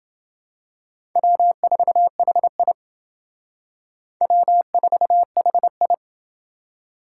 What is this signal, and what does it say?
Morse code, heard as a keyed tone, about 28 wpm, 700 Hz. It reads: W4HI W4HI